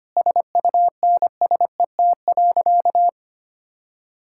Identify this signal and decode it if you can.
Morse code, heard as a keyed tone, about 25 wpm, 690 Hz. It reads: SUNSET.
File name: Easy Sentences with Hints - 25wpm